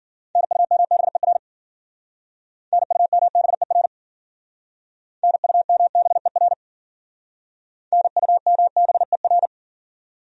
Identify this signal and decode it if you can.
Morse code, heard as a keyed tone, about 52 wpm, 700 Hz. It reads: NUMBER NUMBER NUMBER NUMBER